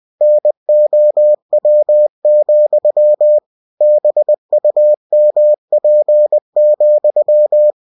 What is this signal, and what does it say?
Morse code, heard as a keyed tone, about 20 wpm, 600 Hz. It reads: NOW, BUMP,
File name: cw-books / winnie-the-pooh / winnie-20wpm-page-0000